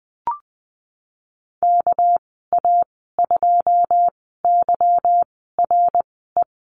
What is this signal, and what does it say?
Morse code, heard as a keyed tone, about 20 wpm, spaced as for 15 wpm, 700 Hz. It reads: KA2YRE
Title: 200 Lesson 32g - Letter k - Call Signs 20wpm